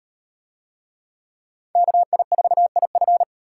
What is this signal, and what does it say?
Morse code, heard as a keyed tone, about 37 wpm, 700 Hz. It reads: KI4IF